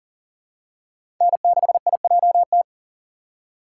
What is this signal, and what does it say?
Morse code, heard as a keyed tone, about 40 wpm, 700 Hz. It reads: N6IJT
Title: USA Call Signs - 5 Characters - 4x Character Spacing - 40wpm